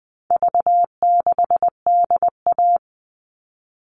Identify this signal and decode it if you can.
Morse code, heard as a keyed tone, about 20 wpm, 700 Hz. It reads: V6DA
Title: All Cut Numbers - 4-Digit 20wpm